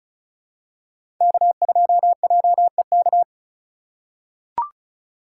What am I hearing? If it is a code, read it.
Morse code, heard as a keyed tone, about 36 wpm, 700 Hz. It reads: K2JEK
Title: USA Call Signs - 3x Character Spacing - 35wpm